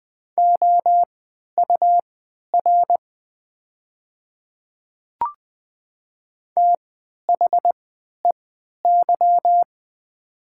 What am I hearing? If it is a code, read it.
Morse code, heard as a keyed tone, about 20 wpm, spaced as for 11 wpm, 700 Hz. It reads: OUR THEY
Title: 3x Character Spacing - Top 200 Words - 20wpm